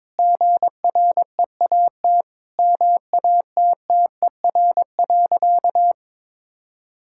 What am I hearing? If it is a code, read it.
Morse code, heard as a keyed tone, about 22 wpm, 700 Hz. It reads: GREAT MATTER.
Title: Sentences from Top 1600 Words 22wpm